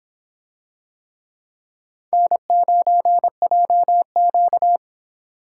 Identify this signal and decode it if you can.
Morse code, heard as a keyed tone, about 26 wpm, 700 Hz. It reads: N9JQ